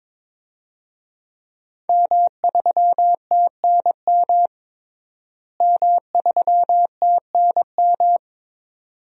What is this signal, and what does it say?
Morse code, heard as a keyed tone, about 22 wpm, 700 Hz. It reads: M3TNM M3TNM
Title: DXCC-100 Call Signs - 5 Characters - Repeated 2x - 22wpm